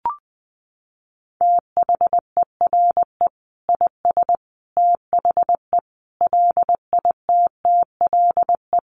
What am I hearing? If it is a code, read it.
Morse code, heard as a keyed tone, about 20 wpm, 700 Hz. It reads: THERE IS THE LITTLE
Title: Sentences from Top 200 Words 20wpm